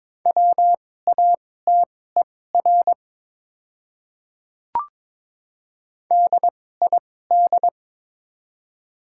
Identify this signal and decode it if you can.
Morse code, heard as a keyed tone, about 22 wpm, spaced as for 16 wpm, 700 Hz. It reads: WATER DID
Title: 2x Character Spacing - Top 100 Words - 22wpm